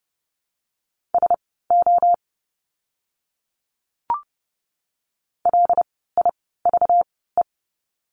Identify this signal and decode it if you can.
Morse code, heard as a keyed tone, about 30 wpm, spaced as for 17 wpm, 700 Hz. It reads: SO LIVE
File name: course / 30:10wpm / 237 Lesson 38f - Letter q - Words 30wpm